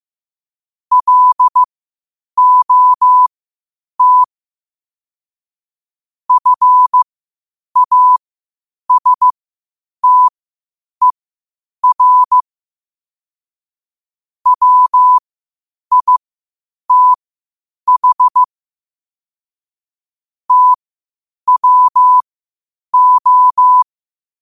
Morse code, heard as a keyed tone, about 15 wpm, spaced as for 8 wpm, 1000 Hz. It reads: LOT FASTER WITH TWO
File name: qotdfiles / 8wpm